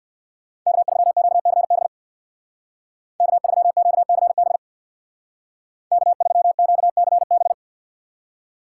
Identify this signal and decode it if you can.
Morse code, heard as a keyed tone, about 56 wpm, 700 Hz. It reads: K3QCB K3QCB K3QCB
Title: Speed Racing - USA Call Signs 50wpm